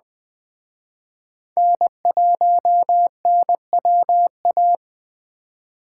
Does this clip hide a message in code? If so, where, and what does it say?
Yes: Morse code from 1.6 to 4.8 s, heard as a keyed tone, about 21 wpm, 700 Hz. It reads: N1NWA